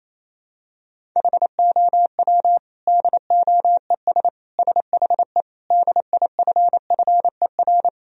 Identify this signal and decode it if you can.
Morse code, heard as a keyed tone, about 28 wpm, 700 Hz. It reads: HOW DOES SHE DIFFER
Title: Sentences from Top 1000 Words - Review Part 1 - 28wpm